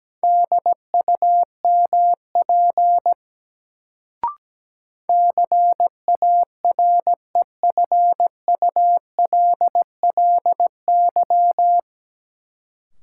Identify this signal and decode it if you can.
Morse code, heard as a keyed tone, about 17 wpm, 700 Hz. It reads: DUMP CAREFULLY
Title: Sentences from Top 1300 Words - New Word Drill Alt - 17wpm